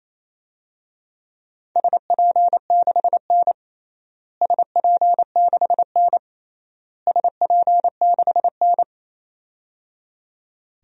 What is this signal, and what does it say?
Morse code, heard as a keyed tone, about 27 wpm, 700 Hz. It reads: SP6N SP6N SP6N
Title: DXCC-100 Call Signs - 4 Characters - Repeated 3x - 28wpm